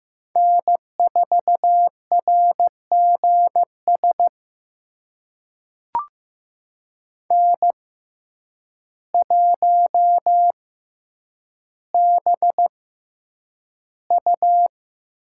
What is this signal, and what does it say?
Morse code, heard as a keyed tone, about 15 wpm, spaced as for 8 wpm, 700 Hz. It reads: N4RGS N1BU
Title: USA Call Signs - 5 Characters - 6x Character Spacing - 15wpm